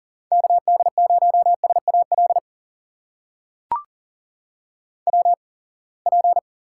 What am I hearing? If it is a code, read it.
Morse code, heard as a keyed tone, about 39 wpm, spaced as for 27 wpm, 700 Hz. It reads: KD0SAL WP
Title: USA Call Signs - 6 Characters - 8x Character Spacing - 40wpm